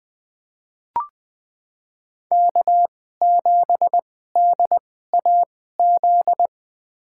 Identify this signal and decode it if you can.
Morse code, heard as a keyed tone, about 21 wpm, spaced as for 15 wpm, 700 Hz. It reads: K7DAZ
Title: USA Call Signs - 5 Characters - 2x Character Spacing - 20wpm